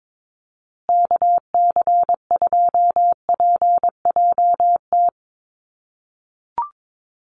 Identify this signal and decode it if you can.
Morse code, heard as a keyed tone, about 22 wpm, 700 Hz. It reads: KC2PJT
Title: USA Call Signs Repeated 2x 22wpm